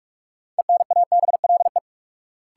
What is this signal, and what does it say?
Morse code, heard as a keyed tone, about 46 wpm, 700 Hz. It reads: ENABLE
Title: Sentences from Top 1500 Words - New Word Drill - 45wpm